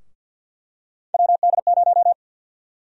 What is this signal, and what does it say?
Morse code, heard as a keyed tone, about 48 wpm, 700 Hz. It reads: WD0
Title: USA Call Sign Prefixes - Repeated 2x - 50wpm